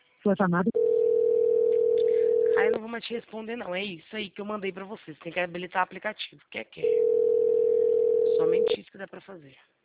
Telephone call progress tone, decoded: ringback tone

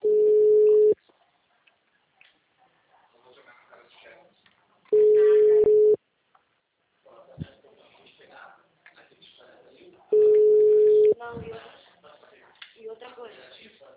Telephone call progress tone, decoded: ringback tone

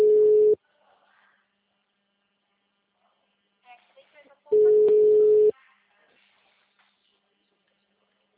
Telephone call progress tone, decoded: ringback tone

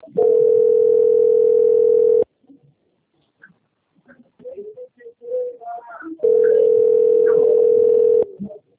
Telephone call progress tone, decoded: ringback tone

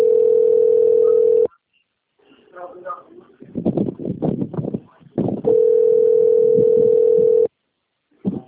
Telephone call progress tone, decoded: ringback tone

